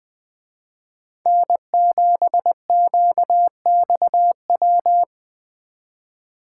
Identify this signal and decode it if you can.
Morse code, heard as a keyed tone, about 23 wpm, 700 Hz. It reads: N7QXW
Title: Speed Racing - USA 1x3 Call Signs 20wpm